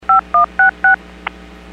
DTMF, heard as telephone touch tones, tones of about 130 ms. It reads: 5166
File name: WPS / 02398 06-47 WPS 451.25 WPS